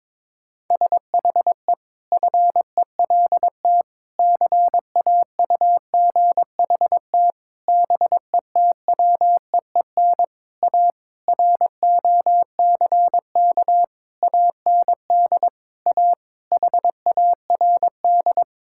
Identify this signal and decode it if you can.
Morse code, heard as a keyed tone, about 22 wpm, 700 Hz. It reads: SHE FELT CAUGHT BETWEEN A ROCK AND A HARD
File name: practice / 22wpm / Sentences with 8-Word Phrases - 22wpm